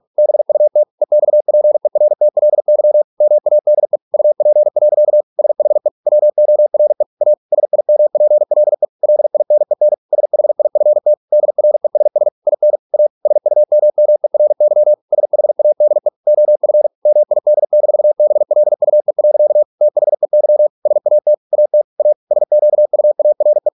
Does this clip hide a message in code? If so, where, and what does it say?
Yes: Morse code from 0.2 to 23.7 s, heard as a keyed tone, about 46 wpm, 600 Hz. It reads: BUT EXPERTLY MADE UP. SHE WORE A SIMPLE LINEN SHIFT DRESS IN A SUMMERY SHADE OF MID-BLUE. THEY SAT AT A SQUARE